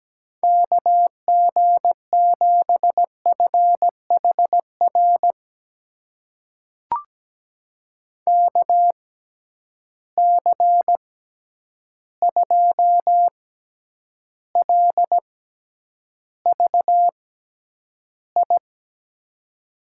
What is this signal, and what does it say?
Morse code, heard as a keyed tone, about 17 wpm, spaced as for 9 wpm, 700 Hz. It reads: KG7FHR KC2LVI
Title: USA Call Signs - 6 Characters - 6x Character Spacing - 17wpm